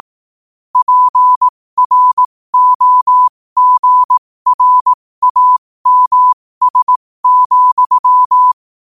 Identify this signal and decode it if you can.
Morse code, heard as a keyed tone, about 18 wpm, spaced as for 15 wpm, 1000 Hz. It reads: PROGRAMS,
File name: qotdfiles / 15wpm